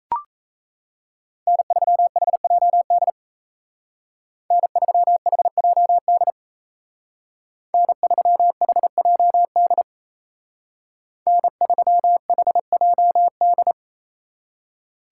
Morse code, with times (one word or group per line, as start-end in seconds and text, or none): 1.5-3.1 N3HJD
4.5-6.3 N3HJD
7.7-9.8 N3HJD
11.3-13.7 N3HJD